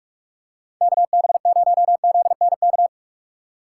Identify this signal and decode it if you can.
Morse code, heard as a keyed tone, about 45 wpm, 700 Hz. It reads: KB0ZNK